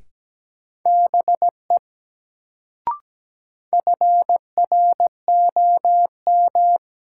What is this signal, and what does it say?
Morse code, heard as a keyed tone, about 17 wpm, 700 Hz. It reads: BE FROM
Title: Sentences from Top 100 Words - New Word Drill - 17wpm